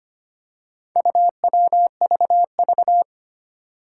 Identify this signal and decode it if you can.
Morse code, heard as a keyed tone, about 25 wpm, 700 Hz. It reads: UWVV
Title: Four Challenging Letter-Number Combinations 25wpm